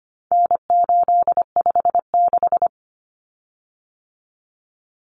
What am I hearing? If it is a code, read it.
Morse code, heard as a keyed tone, about 25 wpm, 700 Hz. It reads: N856
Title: Common Cut Numbers - 4-Digit 25wpm